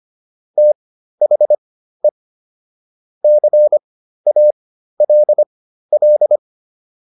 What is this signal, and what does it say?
Morse code, heard as a keyed tone, about 25 wpm, spaced as for 13 wpm, 600 Hz. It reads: THE CALL